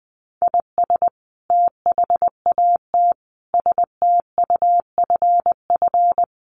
Morse code, heard as a keyed tone, about 19 wpm, 700 Hz. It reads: IS THAT STUFF
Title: Sentences from Top 1400 Words 20wpm